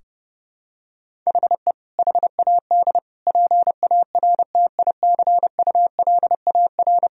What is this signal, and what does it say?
Morse code, heard as a keyed tone, about 32 wpm, 700 Hz. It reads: HE HAD PARTICULAR
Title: Sentences from Top 1000 Words - Review Part 4 - 30wpm